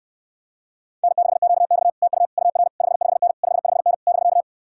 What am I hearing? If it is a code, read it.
Morse code, heard as a keyed tone, about 69 wpm, 700 Hz. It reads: N6QX TU UR 55N 55N <BK>